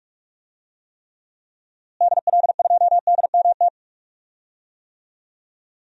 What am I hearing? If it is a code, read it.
Morse code, heard as a keyed tone, about 47 wpm, 700 Hz. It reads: DL2DMT
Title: DXCC-100 Call Signs - 6 Characters - Repeated 2x - 45wpm